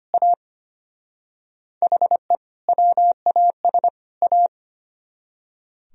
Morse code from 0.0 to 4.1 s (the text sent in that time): A HE WAS